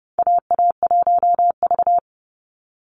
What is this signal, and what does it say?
Morse code, heard as a keyed tone, about 30 wpm, 700 Hz. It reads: AA1V
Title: Call Signs - 30wpm